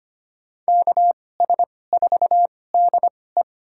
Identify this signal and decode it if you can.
Morse code, heard as a keyed tone, about 25 wpm, spaced as for 18 wpm, 700 Hz. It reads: KS4DE
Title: USA Call Signs - 5 Characters - 2x Character Spacing - 25wpm